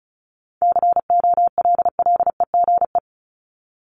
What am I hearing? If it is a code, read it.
Morse code, heard as a keyed tone, about 34 wpm, 700 Hz. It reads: COLLEGE
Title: Top 1001-1250 Words 35wpm